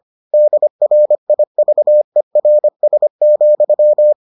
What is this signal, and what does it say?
Morse code, heard as a keyed tone, about 25 wpm, 600 Hz. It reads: DRIVERS,